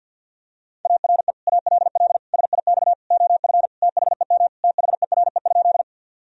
Morse code, heard as a keyed tone, about 47 wpm, 700 Hz. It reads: ARE ALL SIX OF THEM THERE?
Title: Sentences from Top 1000 Words - Review Part 2 - 50wpm